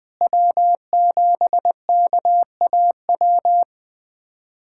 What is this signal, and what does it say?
Morse code, heard as a keyed tone, about 20 wpm, 700 Hz. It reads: W7KAW
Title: USA Callsign DE Callsign - 5 Characters - 20wpm